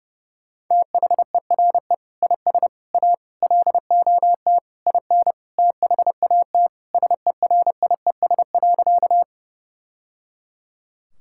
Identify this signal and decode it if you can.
Morse code, heard as a keyed tone, about 32 wpm, 700 Hz. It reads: THERE IS A LOT IN THAT SERIES.
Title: Sentences from Top 1000 Words - Review Part 5 - 30wpm